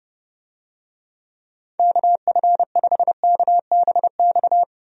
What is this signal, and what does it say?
Morse code, heard as a keyed tone, about 32 wpm, 700 Hz. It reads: KF5KBX
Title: USA Call Signs - 6 Characters - Repeated 2x - 30wpm